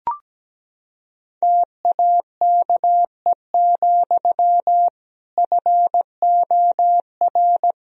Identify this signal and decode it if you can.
Morse code, heard as a keyed tone, about 17 wpm, 700 Hz. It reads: TAKE, FOR
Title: Sentences from Top 1000 Words - Review Part 1 - 17wpm